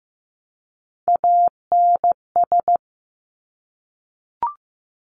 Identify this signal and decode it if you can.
Morse code, heard as a keyed tone, about 15 wpm, 700 Hz. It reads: ANS
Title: Top Three Letter Combinations in Words 15wpm